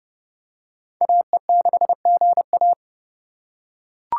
Morse code, heard as a keyed tone, about 28 wpm, 700 Hz. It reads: AE6GA